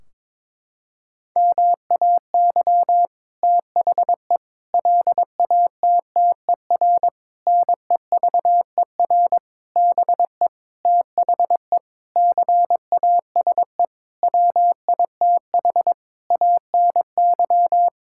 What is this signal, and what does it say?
Morse code, heard as a keyed tone, about 22 wpm, 700 Hz. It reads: MAY THE LATTER NEVER BE THE CASE WITH ANY
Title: Sentences from Top 1600 Words 22wpm